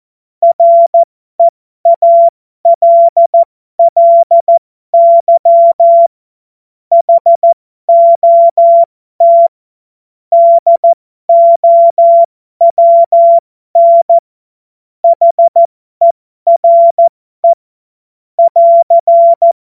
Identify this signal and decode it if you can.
Morse code, heard as a keyed tone, about 14 wpm, spaced as for 12 wpm, 680 Hz. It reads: REALLY HOT DOWN HERE <AR>
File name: Morsekurs / Witze